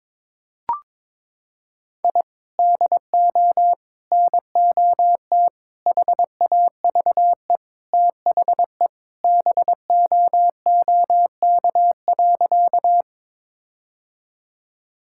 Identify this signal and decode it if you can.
Morse code, heard as a keyed tone, about 23 wpm, 700 Hz. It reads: I DO NOT HAVE THE BOOK.
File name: practice / 22wpm / Sentences with 4-Word Phrases - 22wpm